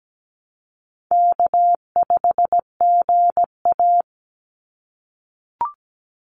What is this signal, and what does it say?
Morse code, heard as a keyed tone, about 17 wpm, 700 Hz. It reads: K5GA
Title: Easy Call Signs - 17wpm